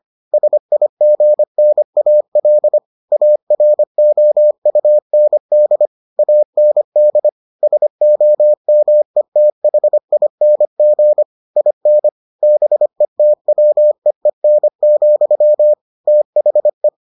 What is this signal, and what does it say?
Morse code, heard as a keyed tone, about 25 wpm, 600 Hz. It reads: SIGNAL AROUND AND SOMETHING IN BETWEEN, THE